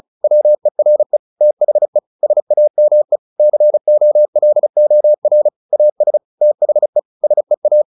Morse code, heard as a keyed tone, about 34 wpm, 600 Hz. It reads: WERE THE SAME COLOR AS THE SEA
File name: the old man and the sea / the old man and the sea 1 35wpm